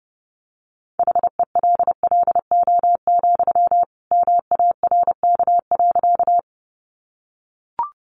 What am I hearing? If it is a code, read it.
Morse code, heard as a keyed tone, about 30 wpm, 700 Hz. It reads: HELLO, MARK.
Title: Sentences from Top 1100 Words 30wpm